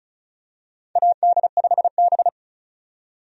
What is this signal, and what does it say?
Morse code, heard as a keyed tone, about 36 wpm, 700 Hz. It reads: AD5B
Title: USA Call Signs - 4 Characters - 35wpm